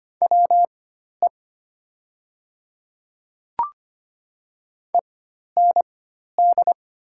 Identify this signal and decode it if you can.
Morse code, heard as a keyed tone, about 25 wpm, spaced as for 12 wpm, 700 Hz. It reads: WE END
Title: 4x Character Spacing - Top 200 Words - 25wpm